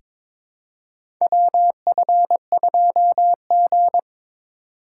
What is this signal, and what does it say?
Morse code, heard as a keyed tone, about 22 wpm, 700 Hz. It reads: WF2G
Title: USA Call Signs - 4 Characters - 3x Character Spacing - 22wpm